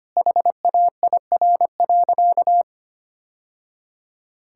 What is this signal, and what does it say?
Morse code, heard as a keyed tone, about 25 wpm, 700 Hz. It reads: HAIR.
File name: Easy Sentences with Hints - 25wpm